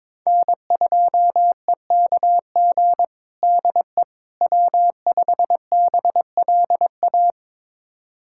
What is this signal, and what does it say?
Morse code, heard as a keyed tone, about 22 wpm, 700 Hz. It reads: N2EKG DE W5BLA
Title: USA Callsign DE Callsign - 5 Characters - 22wpm